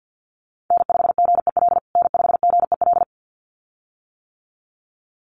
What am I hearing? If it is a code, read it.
Morse code, heard as a keyed tone, about 50 wpm, 700 Hz. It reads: N5GEL N5GEL
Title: USA Call Signs Repeated 2x 50wpm